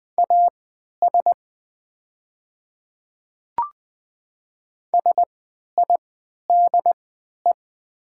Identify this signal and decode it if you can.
Morse code, heard as a keyed tone, about 20 wpm, spaced as for 11 wpm, 700 Hz. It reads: AS SIDE
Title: 3x Character Spacing - Top 100 Words - 20wpm